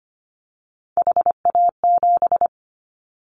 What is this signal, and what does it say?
Morse code, heard as a keyed tone, about 25 wpm, 700 Hz. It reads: HA7
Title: Callsign Prefixes for Top 50 DXCC - 25wpm